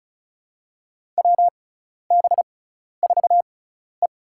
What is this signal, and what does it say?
Morse code, heard as a keyed tone, about 36 wpm, spaced as for 12 wpm, 700 Hz. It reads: WB4E
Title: USA Call Signs - 6x Character Spacing - 35wpm